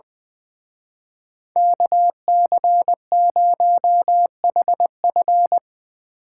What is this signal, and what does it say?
Morse code, heard as a keyed tone, about 22 wpm, 700 Hz. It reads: KC0HF
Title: Speed Racing - USA 2x2 Call Signs 20wpm